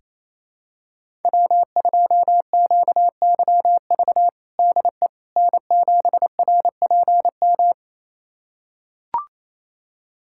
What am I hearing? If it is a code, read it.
Morse code, heard as a keyed tone, about 28 wpm, 700 Hz. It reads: W2QYV DE N7RPM
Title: USA Callsign DE Callsign - 5 Characters - 28wpm